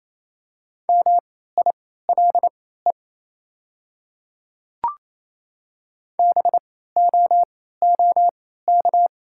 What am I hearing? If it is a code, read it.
Morse code, heard as a keyed tone, about 27 wpm, spaced as for 16 wpm, 700 Hz. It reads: MILE BOOK
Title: 3x Character Spacing - Top 500 Words - 28wpm